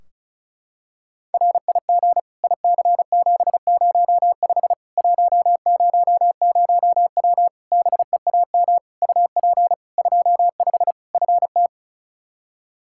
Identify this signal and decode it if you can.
Morse code, heard as a keyed tone, about 36 wpm, 700 Hz. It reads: RIG IC705 100W BEAM UP 25 FT
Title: K1USN SST QSO Practice 35wpm